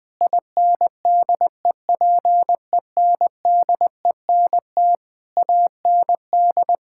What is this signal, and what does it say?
Morse code, heard as a keyed tone, about 21 wpm, 700 Hz. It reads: INDEPENDENT AND
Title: Sentences from Top 1000 Words - Review Part 6 - 20wpm